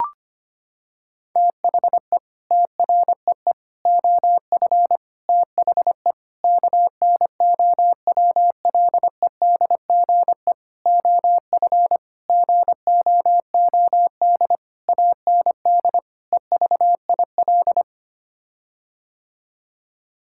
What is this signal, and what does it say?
Morse code, heard as a keyed tone, about 25 wpm, 700 Hz. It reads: THE TREE OF THE KNOWLEDGE OF GOOD AND EVIL
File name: Sets of 9 Words - 25wpm